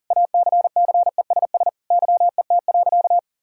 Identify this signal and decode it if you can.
Morse code, heard as a keyed tone, about 40 wpm, 700 Hz. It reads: ACCESS YET.
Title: Sentences from Top 1000 Words - Review Part 4 - 40wpm